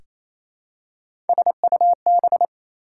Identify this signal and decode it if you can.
Morse code, heard as a keyed tone, about 28 wpm, 700 Hz. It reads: SUB
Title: Prefixes and Suffixes 28wpm